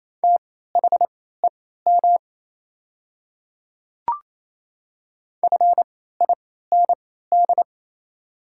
Morse code, heard as a keyed tone, about 27 wpm, spaced as for 16 wpm, 690 Hz. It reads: THEM FIND
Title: 3x Character Spacing - Top 200 Words - 28wpm